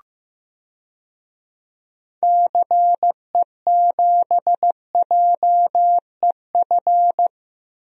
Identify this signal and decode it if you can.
Morse code, heard as a keyed tone, about 15 wpm, 700 Hz. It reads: CE7JEF
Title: DXCC-100 Call Signs - 6 Characters - Repeated 2x - 15wpm